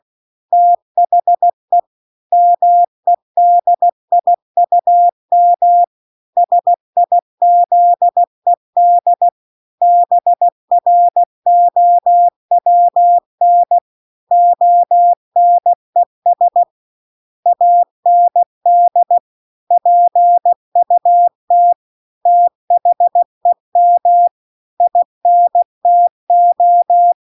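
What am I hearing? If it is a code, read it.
Morse code, heard as a keyed tone, about 16 wpm, 700 Hz. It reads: THE MEDIUM SIZED BROWN ONES AND PUT THEM INTO
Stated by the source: Morse practice text at 16 WPM.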